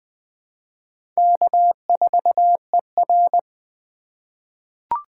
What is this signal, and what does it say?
Morse code, heard as a keyed tone, about 20 wpm, 700 Hz. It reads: K4ER
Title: USA Call Signs - 4 Characters - 4x Character Spacing - 20wpm